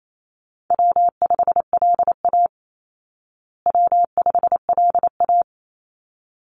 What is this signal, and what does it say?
Morse code, heard as a keyed tone, about 28 wpm, 700 Hz. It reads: W5LA W5LA
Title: Call Signs Repeated 3x - 28wpm